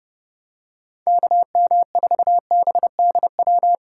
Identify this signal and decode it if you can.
Morse code, heard as a keyed tone, about 30 wpm, 700 Hz. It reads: KM4BDW